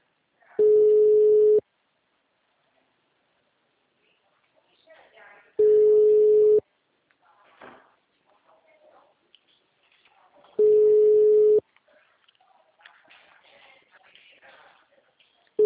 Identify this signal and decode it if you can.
Ringback tone (Europe, 425 Hz), with speech.
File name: out- -2004- - - .4964